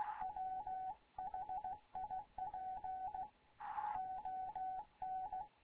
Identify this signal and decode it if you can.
Morse code, heard as a keyed tone, about 16 wpm, 750 Hz. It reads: WHIP ON
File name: Morze-HA2MN / hangp / 2 11 GYAKORLAT